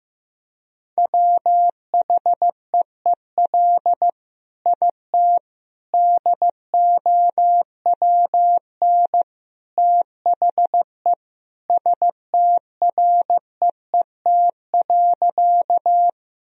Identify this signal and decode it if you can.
Morse code, heard as a keyed tone, about 15 wpm, 700 Hz. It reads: WHEEL IT DOWN THE STREET.
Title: Sentences from Top 1000 Words - Review Part 3 - 15wpm